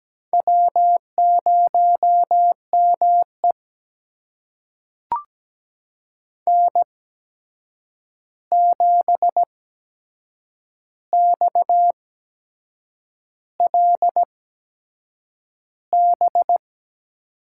Morse code, heard as a keyed tone, about 17 wpm, spaced as for 7 wpm, 700 Hz. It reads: W0ME N7XLB
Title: USA Call Signs - 8x Character Spacing - 17wpm